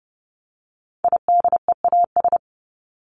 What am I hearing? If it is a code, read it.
Morse code, heard as a keyed tone, about 30 wpm, 700 Hz. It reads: IDEAS
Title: ES:S Suffix - Top 100 Words with Base Words 30wpm